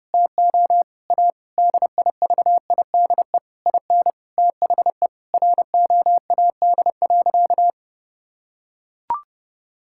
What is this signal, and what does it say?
Morse code, heard as a keyed tone, about 32 wpm, 700 Hz. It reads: TO A DIVIDE IN THE ROAD.
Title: Sentences from Top 1600 Words 30wpm